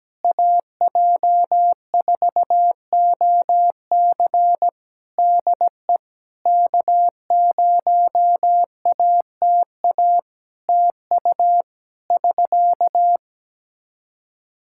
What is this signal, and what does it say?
Morse code, heard as a keyed tone, about 17 wpm, 700 Hz. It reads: AJ4OC DE K0ATA TU <SK>
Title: Practice QSO - Exchange 4 - Ending - 17wpm